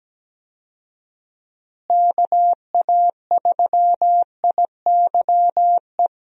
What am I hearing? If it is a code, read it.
Morse code, heard as a keyed tone, about 17 wpm, 700 Hz. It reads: KA3IYE
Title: USA Call Signs - 6 Characters - Repeated 2x - 17wpm